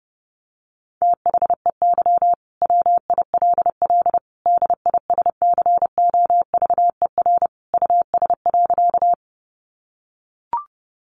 Morse code, heard as a keyed tone, about 30 wpm, 700 Hz. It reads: THEY WILL DISCOVER US.
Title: Sentences from Top 1100 Words 30wpm